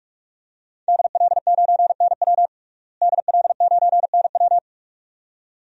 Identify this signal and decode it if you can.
Morse code, heard as a keyed tone, about 46 wpm, 700 Hz. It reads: DL9NW DL9NW
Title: DXCC-100 Call Signs - 5 Characters - Repeated 2x - 45wpm